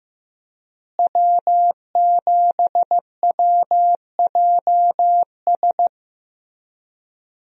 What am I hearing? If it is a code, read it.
Morse code, heard as a keyed tone, about 16 wpm, 700 Hz. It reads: W7WJS
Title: USA Call Signs - 5 Characters - Repeated 3x - 15wpm